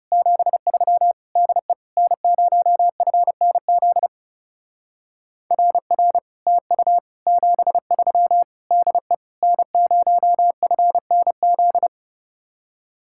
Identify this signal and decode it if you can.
Morse code, heard as a keyed tone, about 38 wpm, 700 Hz. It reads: 73 DE N0FNZ RR TU 73 DE N0FNZ